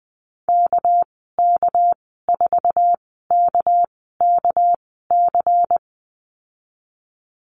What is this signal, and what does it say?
Morse code, heard as a keyed tone, about 19 wpm, spaced as for 14 wpm, 700 Hz. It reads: KK4KKC
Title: 207 Lesson 33g - Letter j - Call Signs 20wpm